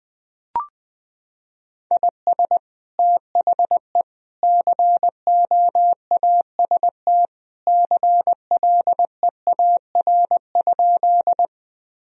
Morse code, heard as a keyed tone, about 20 wpm, 700 Hz. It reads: IS THE COAST CLEAR?
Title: Sentences from Top 1000 Words - Review Part 4 - 20wpm